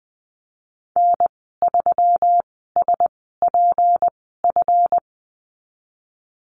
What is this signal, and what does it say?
Morse code, heard as a keyed tone, about 20 wpm, spaced as for 15 wpm, 700 Hz. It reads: N3SPF
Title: 161 Lesson 26g - Letter g - Call Signs 20wpm